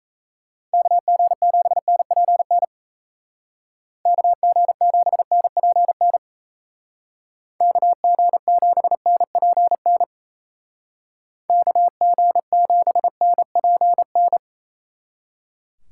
Morse code, heard as a keyed tone, about 33 wpm, 700 Hz. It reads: KG7NPN KG7NPN KG7NPN KG7NPN
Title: Speed Racing - USA 2x3 Call Signs 28wpm